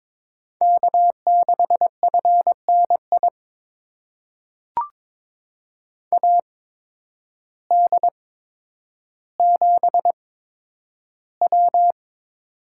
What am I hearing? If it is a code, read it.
Morse code, heard as a keyed tone, about 22 wpm, spaced as for 10 wpm, 700 Hz. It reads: K6FNI AD7W
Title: USA Call Signs - 8x Character Spacing - 22wpm